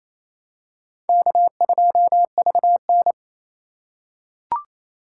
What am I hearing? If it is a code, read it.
Morse code, heard as a keyed tone, about 27 wpm, 700 Hz. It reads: K2VN
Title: USA Call Signs - 2x Character Spacing - 28wpm